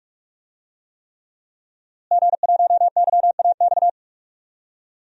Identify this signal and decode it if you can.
Morse code, heard as a keyed tone, about 46 wpm, 700 Hz. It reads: G1YAX
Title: DXCC-100 Call Signs - 5 Characters - Repeated 3x - 45wpm